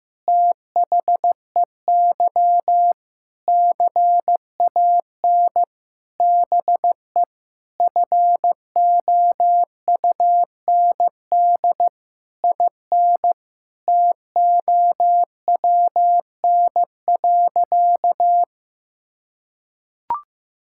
Morse code, heard as a keyed tone, about 15 wpm, 700 Hz. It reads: THEY CAN BE FOUND IN TOWN.
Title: Sentences from Top 1000 Words - Review Part 6 - 15wpm